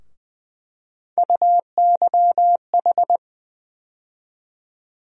Morse code, heard as a keyed tone, about 20 wpm, 700 Hz. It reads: UYH
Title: USA Call Sign Postfixes - 3 Letters 20wpm